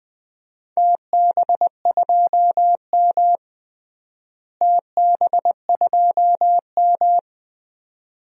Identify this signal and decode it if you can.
Morse code, heard as a keyed tone, about 21 wpm, 700 Hz. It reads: TB2M TB2M
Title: DXCC-100 Call Signs - 4 Characters - Repeated 2x - 20wpm